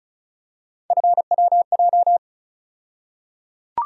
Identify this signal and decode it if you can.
Morse code, heard as a keyed tone, about 34 wpm, 700 Hz. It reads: FWJ
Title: USA Call Sign Postfixes - 3 Letters 35wpm